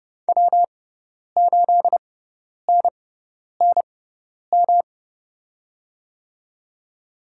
Morse code, heard as a keyed tone, about 31 wpm, spaced as for 10 wpm, 700 Hz. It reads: W8NNM
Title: USA Call Signs - 6x Character Spacing - 30wpm